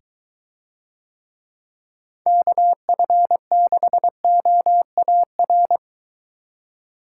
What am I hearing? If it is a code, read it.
Morse code, heard as a keyed tone, about 23 wpm, 700 Hz. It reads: KF6OAR